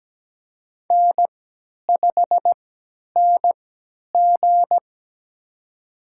Morse code, heard as a keyed tone, about 17 wpm, spaced as for 10 wpm, 700 Hz. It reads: N5NG
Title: USA Call Signs - 4 Characters - 3x Character Spacing - 17wpm